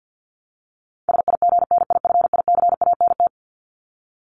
Morse code, heard as a keyed tone, about 50 wpm, 700 Hz. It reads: SIGNIFICANT
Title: LY Suffix - Top 100 Words with Base Words 50wpm